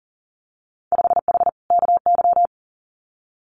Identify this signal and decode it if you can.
Morse code, heard as a keyed tone, about 37 wpm, 700 Hz. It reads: 5H KY
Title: Winter Field Day Exchanges Encore 40wpm